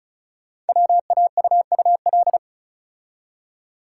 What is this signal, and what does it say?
Morse code, heard as a keyed tone, about 36 wpm, 690 Hz. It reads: WAUUL
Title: Code Groups - 5 Letters - 35wpm